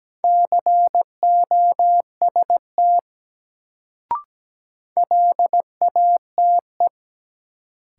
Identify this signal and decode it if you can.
Morse code, heard as a keyed tone, about 17 wpm, 700 Hz. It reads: COST LATE
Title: Sets of 3 Words - New Word Drill - 17wpm